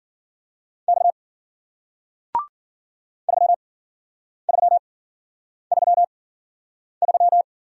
Morse code, heard as a keyed tone, about 54 wpm, spaced as for 9 wpm, 700 Hz. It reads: X 3333